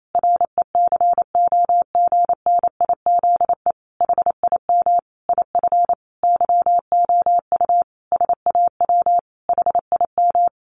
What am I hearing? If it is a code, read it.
Morse code, heard as a keyed tone, about 28 wpm, 700 Hz. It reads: RECOGNIZE HIM IF YOU SAW HIM
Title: Sentences from Top 1400 Words 28wpm